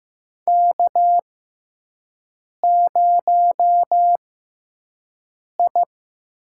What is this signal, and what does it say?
Morse code, heard as a keyed tone, about 15 wpm, spaced as for 5 wpm, 700 Hz. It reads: K0I
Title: USA Call Signs - 6x Character Spacing - 15wpm